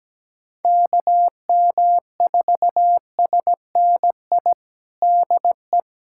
Morse code, heard as a keyed tone, about 17 wpm, 700 Hz. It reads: KM4SNI DE
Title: USA Callsign DE Callsign 17wpm